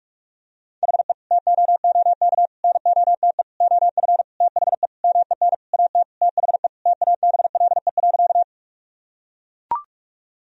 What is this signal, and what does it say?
Morse code, heard as a keyed tone, about 41 wpm, 700 Hz. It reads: HE TOOK NOTE OF THE MEN AT THE TABLE.
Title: Sentences from Top 1000 Words - Review Part 2 - 45wpm